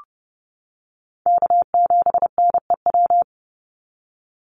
Morse code, heard as a keyed tone, about 30 wpm, 700 Hz. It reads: K7NEW
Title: NA SOTA Chaser Callsigns 30wpm